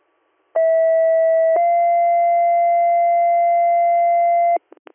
Two-tone page, heard as a one-second tone, then a three-second tone, about 650 then 690 Hz. It is heard from 0.6 to 4.6 s.